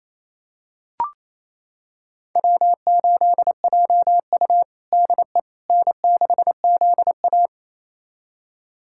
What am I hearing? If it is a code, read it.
Morse code, heard as a keyed tone, about 29 wpm, 700 Hz. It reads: W8JU DE N6ZA
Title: USA Callsign DE Callsign - 4 Characters - 28wpm